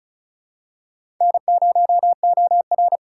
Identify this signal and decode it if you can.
Morse code, heard as a keyed tone, about 34 wpm, 700 Hz. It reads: N0OR
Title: USA Call Signs - 4 Characters - 4x Character Spacing - 35wpm